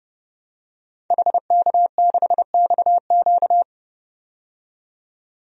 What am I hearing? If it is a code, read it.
Morse code, heard as a keyed tone, about 30 wpm, 700 Hz. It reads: HK6XQ
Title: DXCC-100 Call Signs - 5 Characters - 30wpm